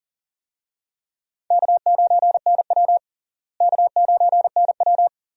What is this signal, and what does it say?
Morse code, heard as a keyed tone, about 40 wpm, 700 Hz. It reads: K9NW K9NW
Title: USA Call Signs - 4 Characters - Repeated 2x - 40wpm